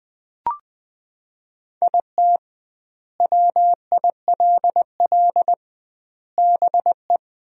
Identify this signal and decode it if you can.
Morse code, heard as a keyed tone, about 20 wpm, 700 Hz. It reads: IT WILL BE
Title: 2x Word-Spacing - Sentences from Top 300 Words - 20wpm